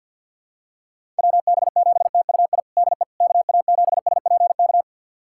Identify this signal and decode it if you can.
Morse code, heard as a keyed tone, about 49 wpm, 700 Hz. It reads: WB7TUI DE KA7SPK
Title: USA Callsign DE Callsign 50wpm